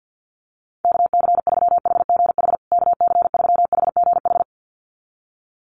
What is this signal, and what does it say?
Morse code, heard as a keyed tone, about 50 wpm, 700 Hz. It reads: KC3HGH KC3HGH
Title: USA Call Signs Repeated 2x 50wpm